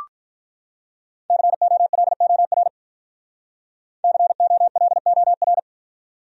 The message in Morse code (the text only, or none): COLOR COLOR